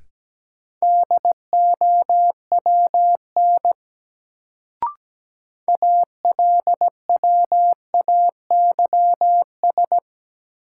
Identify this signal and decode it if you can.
Morse code, heard as a keyed tone, about 17 wpm, 700 Hz. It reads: DOWN ALWAYS
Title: Sentences from Top 300 Words - New Word Drill - 17wpm